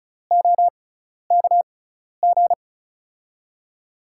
Morse code, heard as a keyed tone, about 34 wpm, spaced as for 12 wpm, 700 Hz. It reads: OKG
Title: USA Call Signs - 6 Characters - 6x Character Spacing - 35wpm